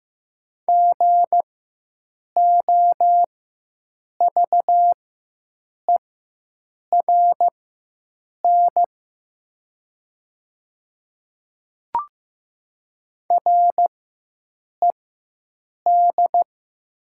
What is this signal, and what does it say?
Morse code, heard as a keyed tone, about 15 wpm, spaced as for 7 wpm, 700 Hz. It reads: GOVERN RED
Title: 4x Character Spacing - Top 500 Words - 15wpm